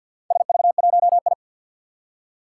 Morse code, heard as a keyed tone, about 52 wpm, 700 Hz. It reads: SV1I
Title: Four Challenging Letter-Number Combinations 50wpm